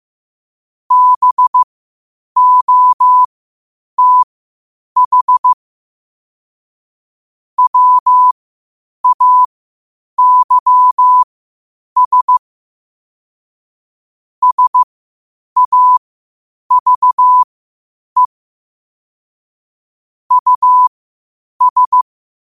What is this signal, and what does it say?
Morse code, heard as a keyed tone, about 15 wpm, spaced as for 8 wpm, 1000 Hz. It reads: BOTH WAYS SAVE US
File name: qotdfiles / 8wpm